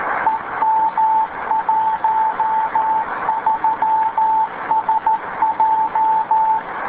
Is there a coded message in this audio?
Yes: Morse code from 0.3 to 6.6 s, heard as a keyed tone, about 13 wpm, 880 Hz, with radio noise. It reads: EM13SJ